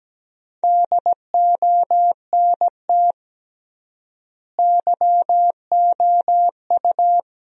Morse code, heard as a keyed tone, about 17 wpm, 700 Hz. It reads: DONT YOU
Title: 3x Word-Spacing - Sentences from Top 300 Words - 17wpm